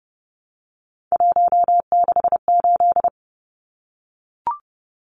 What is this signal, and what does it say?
Morse code, heard as a keyed tone, about 30 wpm, 700 Hz. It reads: 168
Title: Numbers - 3-Digit 30wpm